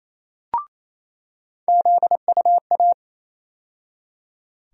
Morse code, heard as a keyed tone, about 28 wpm, 700 Hz. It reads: ZUA
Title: USA Call Sign Postfixes - 3 Letters 28wpm